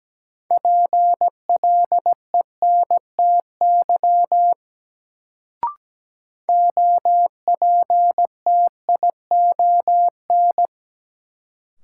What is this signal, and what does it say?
Morse code, heard as a keyed tone, about 17 wpm, 700 Hz. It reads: PLENTY OPTION
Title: Sentences from Top 1300 Words - New Word Drill Alt - 17wpm